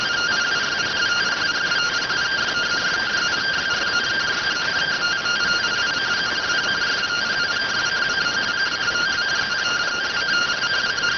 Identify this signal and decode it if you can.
Two-tone teleprinter signal, about 170 Hz shift.